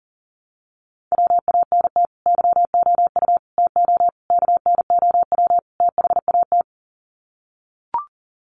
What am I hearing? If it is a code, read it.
Morse code, heard as a keyed tone, about 40 wpm, 700 Hz. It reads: WANT YOU TO KNOW THAT
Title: Sets of 5 Words Encore 40wpm